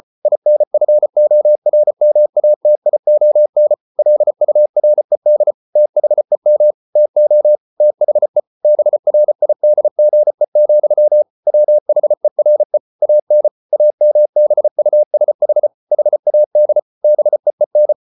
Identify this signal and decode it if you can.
Morse code, heard as a keyed tone, about 34 wpm, 600 Hz. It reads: INFORMATION LURED THEM TO THE BRIDGE, WHERE AN AMBUSH HAD BEEN